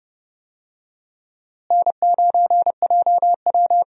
Morse code, heard as a keyed tone, about 30 wpm, 700 Hz. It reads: N9JW
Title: USA Call Signs - 4 Characters - Repeated 3x - 30wpm